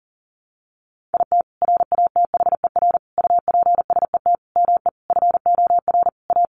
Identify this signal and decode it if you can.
Morse code, heard as a keyed tone, about 40 wpm, 700 Hz. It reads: IT RATHER UPSET ME FOR A
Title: Sentences from Top 900 Words 40wpm